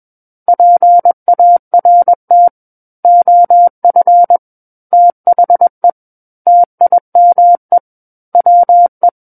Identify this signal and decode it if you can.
Morse code, heard as a keyed tone, about 21 wpm, 700 Hz. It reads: PART OF THE TIME WE
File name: Book One / 12-Chapter 12 - 21 WPM